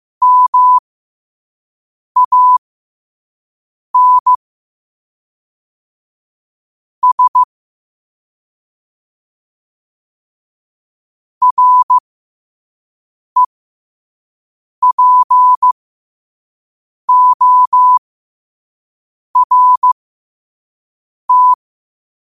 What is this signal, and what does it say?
Morse code, heard as a keyed tone, about 15 wpm, spaced as for 5 wpm, 1000 Hz. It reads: MAN S REPORT
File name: qotdfiles / 5wpm